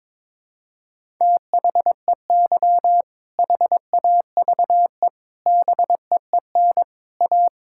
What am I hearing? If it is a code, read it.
Morse code, heard as a keyed tone, about 22 wpm, 700 Hz. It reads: THEY HAVE BEEN A
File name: practice / 22wpm / Sentences with 3-Word Phrases - 22wpm